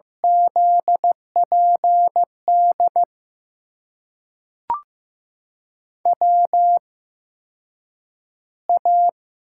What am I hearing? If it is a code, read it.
Morse code, heard as a keyed tone, about 15 wpm, spaced as for 8 wpm, 700 Hz. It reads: ZPD WA
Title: USA Call Signs - 6 Characters - 8x Character Spacing - 15wpm